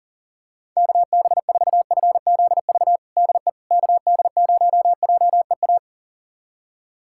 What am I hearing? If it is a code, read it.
Morse code, heard as a keyed tone, about 37 wpm, 700 Hz. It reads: KB4FZV DE KD0JEA
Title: USA Callsign DE Callsign - 6 Characters - 40wpm